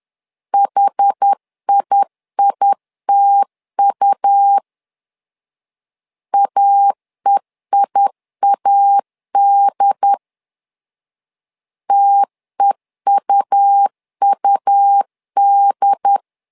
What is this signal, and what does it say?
Morse code, heard as a keyed tone, about 10 wpm, 800 Hz. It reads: HIITU AEIAD TEUUD
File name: LEZIONI / 2 UHDN / 2 UHD 10wpm 1826